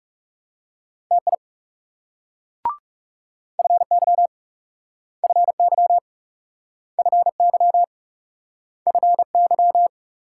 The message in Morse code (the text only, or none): TI FY FY FY FY